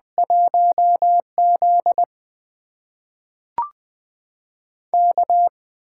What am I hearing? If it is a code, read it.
Morse code, heard as a keyed tone, about 20 wpm, spaced as for 5 wpm, 700 Hz. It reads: 1ZK
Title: USA Call Signs - 8x Character Spacing - 20wpm